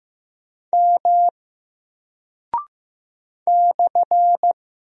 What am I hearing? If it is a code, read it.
Morse code, heard as a keyed tone, about 15 wpm, 700 Hz. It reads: M /
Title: Single Character (Letter:Number:Punctuation:Prosign) Alt 15wpm